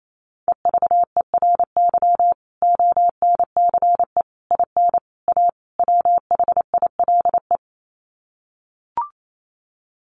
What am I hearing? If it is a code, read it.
Morse code, heard as a keyed tone, about 28 wpm, 700 Hz. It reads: EVERY ONCE IN A WHILE
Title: Sets of 5 Words 28wpm